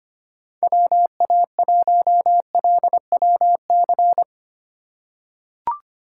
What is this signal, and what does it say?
Morse code, heard as a keyed tone, about 25 wpm, 700 Hz. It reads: WA1LWC
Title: K1USN SST QSO Practice - 25wpm